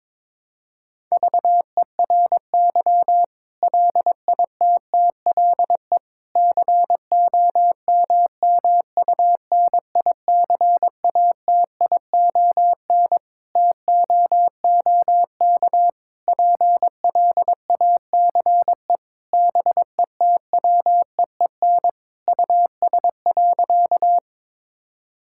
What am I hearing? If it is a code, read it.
Morse code, heard as a keyed tone, about 22 wpm, 700 Hz. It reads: VERY LITTLE COMMUNICATION TOOK PLACE BETWEEN US.
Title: Sentences from Top 1000 Words - Review Part 6 - 22wpm